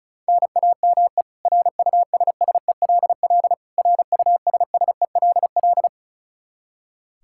Morse code, heard as a keyed tone, about 36 wpm, 700 Hz. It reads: NAME RUSSELL RUSSELL
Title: K1USN SST QSO Practice 35wpm